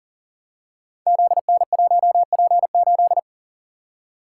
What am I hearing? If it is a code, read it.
Morse code, heard as a keyed tone, about 42 wpm, 700 Hz. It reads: ZN1P8
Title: Code Groups - 5 Letter:Numbers - 40wpm